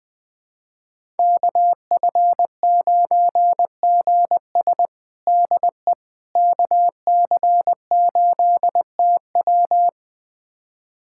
Morse code, heard as a keyed tone, about 20 wpm, 700 Hz. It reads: KF9GS DE KC8TW
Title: USA Callsign DE Callsign - 5 Characters - 20wpm